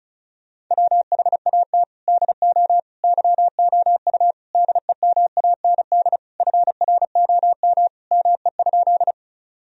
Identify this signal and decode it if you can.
Morse code, heard as a keyed tone, about 36 wpm, 700 Hz. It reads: WHAT DO YOU DEMAND FROM ME?
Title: Sentences from Top 1000 Words - Review Part 4 - 35wpm